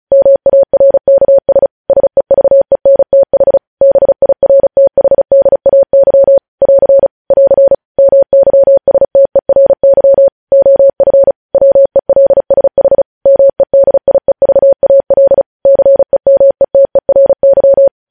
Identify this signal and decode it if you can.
Morse code, heard as a keyed tone, about 36 wpm, 550 Hz. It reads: MARKS SEVENTH BIRTHDAY <AR> <AR> MYSTERY OF WELSH MEDIEVAL CEMETERY
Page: News Headlines in Morse Code at 35 WPM